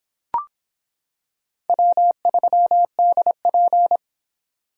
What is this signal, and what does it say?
Morse code, heard as a keyed tone, about 25 wpm, 700 Hz. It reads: W3DP